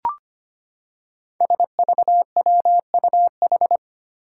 Morse code, heard as a keyed tone, about 25 wpm, 700 Hz. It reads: SVWUH
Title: Five Challenging Letter-Number Combinations 25wpm